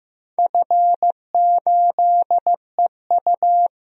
Morse code, heard as a keyed tone, about 15 wpm, 700 Hz. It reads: F8EU
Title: DXCC-100 Call Signs - 4 Characters - Repeated 2x - 15wpm